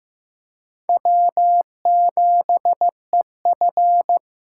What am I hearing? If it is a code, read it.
Morse code, heard as a keyed tone, about 18 wpm, 700 Hz. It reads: W7EF
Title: Speed Racing - USA Call Signs 15wpm